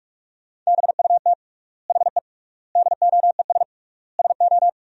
Morse code, heard as a keyed tone, about 46 wpm, 700 Hz. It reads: BUT HE DOES SO
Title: 3x Word-Spacing - Sentences from Top 200 Words - 45wpm